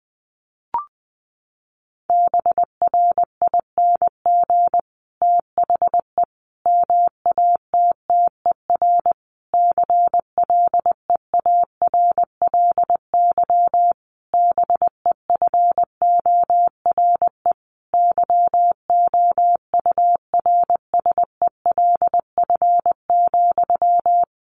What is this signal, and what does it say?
Morse code, heard as a keyed tone, about 20 wpm, 700 Hz. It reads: BRING THE MATTER CLEARLY BEFORE YOURSELF,
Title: Sentences from Top 1300 Words 20wpm